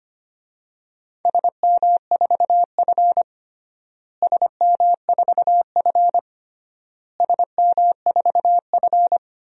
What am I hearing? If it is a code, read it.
Morse code, heard as a keyed tone, about 25 wpm, 700 Hz. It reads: SM4F SM4F SM4F
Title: DXCC-100 Call Signs - 4 Characters - Repeated 3x - 25wpm